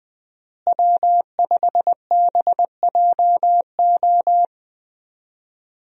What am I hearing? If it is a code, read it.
Morse code, heard as a keyed tone, about 20 wpm, 700 Hz. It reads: W5BJO
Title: USA Call Signs - 5 Characters - Repeated 3x - 20wpm